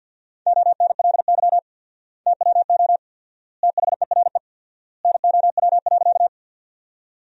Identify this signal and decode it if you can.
Morse code, heard as a keyed tone, about 50 wpm, 700 Hz. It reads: ONLY TWO THERE NOW.
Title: 4x Word-Spacing - Sentences from Top 200 Words - 50wpm